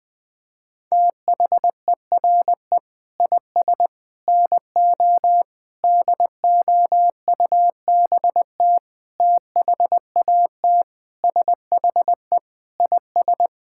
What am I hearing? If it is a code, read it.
Morse code, heard as a keyed tone, about 21 wpm, 700 Hz. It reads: THERE IS NO DOUBT THAT SHE IS
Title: Sentences with 5-Word Phrases - Spoken Beginning - 20wpm